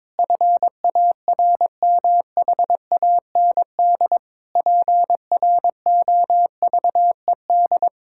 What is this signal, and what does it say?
Morse code, heard as a keyed tone, about 22 wpm, 700 Hz. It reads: FARMHAND PROVED
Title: Pangram Challenge v2 - 22wpm